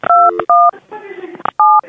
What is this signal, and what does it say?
DTMF, heard as telephone touch tones, tones of about 230 ms, with speech. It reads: 217